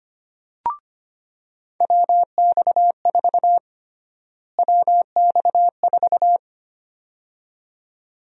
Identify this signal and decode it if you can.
Morse code, heard as a keyed tone, about 25 wpm, 700 Hz. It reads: WX4 WX4
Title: USA Call Sign Prefixes - Repeated 2x - 25wpm